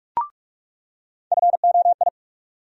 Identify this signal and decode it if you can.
Morse code, heard as a keyed tone, about 47 wpm, 700 Hz. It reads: FOI
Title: USA Call Sign Postfixes - 3 Letters 45wpm